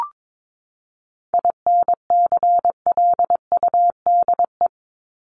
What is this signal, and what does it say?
Morse code, heard as a keyed tone, about 22 wpm, 700 Hz. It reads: INCLUDE
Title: ED Suffix - Top 100 Words with Base Words 22wpm